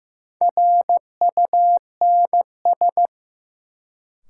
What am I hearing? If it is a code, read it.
Morse code, heard as a keyed tone, about 16 wpm, 700 Hz. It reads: RUNS
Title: Sentences from Top 600 Words - New Word Drill - 15wpm